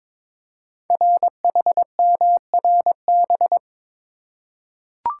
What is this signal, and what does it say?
Morse code, heard as a keyed tone, about 22 wpm, 700 Hz. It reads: RHMRB
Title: Code Groups - 5 Letters - 22wpm